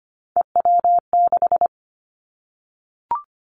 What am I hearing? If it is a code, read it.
Morse code, heard as a keyed tone, about 25 wpm, 700 Hz. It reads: EW6
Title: Callsign Prefixes for Top 50 DXCC - 25wpm